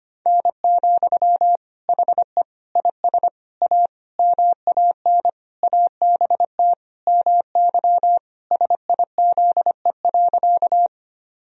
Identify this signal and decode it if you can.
Morse code, heard as a keyed tone, about 25 wpm, 700 Hz. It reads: N, HE IS A MAN ABT MY SIZE.
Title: Sentences from Top 400 Words and Abbreviations 25wpm